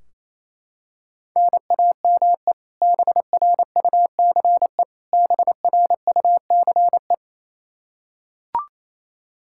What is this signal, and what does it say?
Morse code, heard as a keyed tone, about 28 wpm, 700 Hz. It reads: NAME BRUCE BRUCE
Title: Practice QSO - Exchange 1 - Beginning - 28wpm